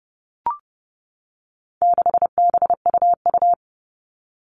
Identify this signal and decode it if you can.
Morse code, heard as a keyed tone, about 30 wpm, 700 Hz. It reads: 6BUU
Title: All Cut Numbers - 4-Digit 30wpm